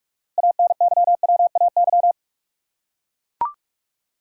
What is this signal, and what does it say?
Morse code, heard as a keyed tone, about 44 wpm, 700 Hz. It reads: ANYWAY
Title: Six Letter Words - 45wpm